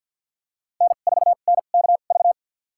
Morse code, heard as a keyed tone, about 50 wpm, spaced as for 36 wpm, 700 Hz. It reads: N4NKV
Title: USA Call Signs - 5 Characters - 2x Character Spacing - 50wpm